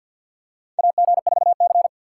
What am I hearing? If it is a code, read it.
Morse code, heard as a keyed tone, about 52 wpm, 700 Hz. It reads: AG4C
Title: USA Call Signs - 4 Characters - 6x Character Spacing - 50wpm